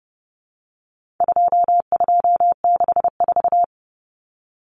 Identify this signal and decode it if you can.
Morse code, heard as a keyed tone, about 30 wpm, 700 Hz. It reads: 2264
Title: Numbers - 4-Digit 30wpm